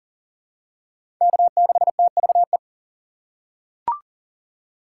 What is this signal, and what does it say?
Morse code, heard as a keyed tone, about 40 wpm, 700 Hz. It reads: K6TVE
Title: USA Call Signs - 2x Character Spacing - 40wpm